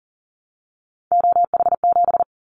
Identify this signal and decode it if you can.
Morse code, heard as a keyed tone, about 40 wpm, 700 Hz. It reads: OH7
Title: Callsign Prefixes for Top 25 DXCC - 40wpm